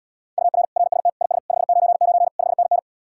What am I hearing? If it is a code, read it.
Morse code, heard as a keyed tone, about 77 wpm, 700 Hz. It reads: UR RST IS 599 5NN